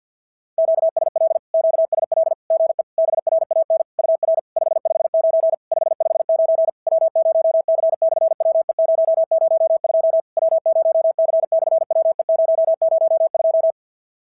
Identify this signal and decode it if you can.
Morse code, heard as a keyed tone, about 50 wpm, 650 Hz. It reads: QSL QSL GE BRAN UR 559 559 W0C/WE002 W0C/WE002
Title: SOTA QSO Practice 50wpm